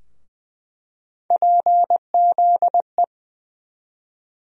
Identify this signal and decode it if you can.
Morse code, heard as a keyed tone, about 20 wpm, 700 Hz. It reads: PZE
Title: USA Call Sign Postfixes - 3 Letters 20wpm